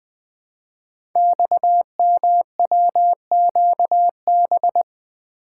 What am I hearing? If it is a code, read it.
Morse code, heard as a keyed tone, about 20 wpm, 700 Hz. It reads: XMWQB
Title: Code Groups - 5 Letter:Numbers - 20wpm